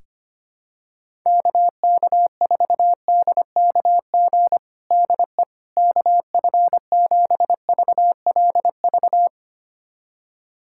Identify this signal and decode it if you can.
Morse code, heard as a keyed tone, about 25 wpm, 700 Hz. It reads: KK4DKG DE KF7VLV
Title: USA Callsign DE Callsign 25wpm